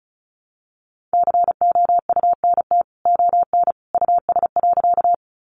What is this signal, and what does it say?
Morse code, heard as a keyed tone, about 34 wpm, 700 Hz. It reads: COUNT ON US.
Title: Sentences from Top 1500 Words 35wpm